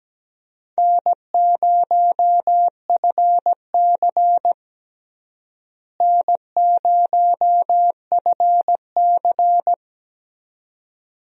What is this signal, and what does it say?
Morse code, heard as a keyed tone, about 17 wpm, 700 Hz. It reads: N0FC N0FC
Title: USA Call Signs - 4 Characters - Repeated 2x - 17wpm